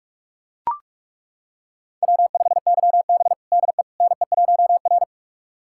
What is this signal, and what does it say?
Morse code, heard as a keyed tone, about 42 wpm, 700 Hz. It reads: W5YB DE NE1R
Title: USA Callsign DE Callsign - 4 Characters - 45wpm